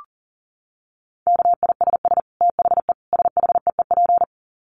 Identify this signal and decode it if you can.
Morse code, heard as a keyed tone, about 40 wpm, 700 Hz. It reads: KISS THE SHEEP
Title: Dit Challenge 40wpm